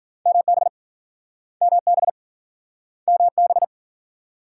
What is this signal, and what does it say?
Morse code, heard as a keyed tone, about 50 wpm, 700 Hz. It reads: MB MB MB
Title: Speed-Racing Multi-letter Phonemes - 40wpm